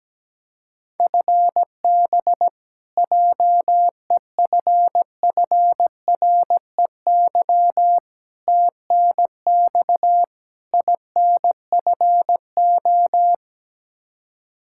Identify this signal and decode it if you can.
Morse code, heard as a keyed tone, about 17 wpm, 700 Hz. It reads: FB JEFFREY TNX INFO
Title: Practice QSO - Exchange 3 - More Info - 17wpm